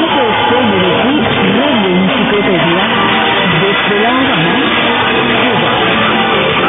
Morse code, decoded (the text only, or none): none